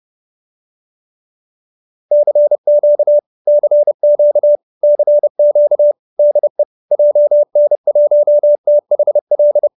Morse code, heard as a keyed tone, about 30 wpm, 590 Hz. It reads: CQ CQ CQ DE JN1THL
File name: QSO309 30